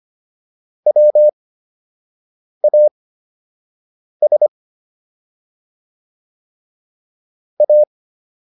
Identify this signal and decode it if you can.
Morse code, heard as a keyed tone, about 25 wpm, spaced as for 6 wpm, 600 Hz. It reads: WAS A